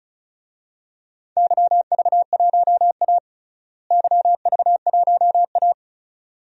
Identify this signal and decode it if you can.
Morse code, heard as a keyed tone, about 36 wpm, 700 Hz. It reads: YV1A YV1A
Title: DXCC-100 Call Signs - 4 Characters - Repeated 2x - 35wpm